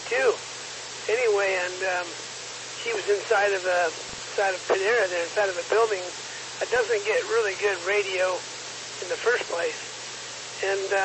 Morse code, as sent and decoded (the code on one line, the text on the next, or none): none